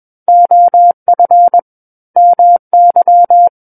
Morse code, heard as a keyed tone, about 21 wpm, 700 Hz. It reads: OF MY
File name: Book One / 12-Chapter 12 - 21 WPM